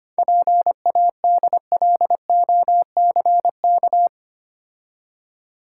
Morse code, heard as a keyed tone, about 25 wpm, 700 Hz. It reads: PADLOCK
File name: Easy Words - Speed-Racing with Hints - 25wpm